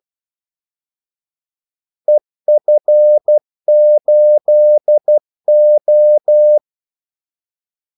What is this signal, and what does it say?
Morse code, heard as a keyed tone, about 12 wpm, 600 Hz. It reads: EF8O
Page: 2 ekstra ordmellemrum mellem kaldesignalerne